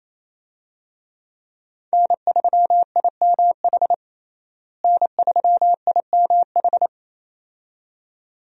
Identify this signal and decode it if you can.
Morse code, heard as a keyed tone, about 28 wpm, 700 Hz. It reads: N3IMH N3IMH
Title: USA Call Signs - 5 Characters - Repeated 2x - 28wpm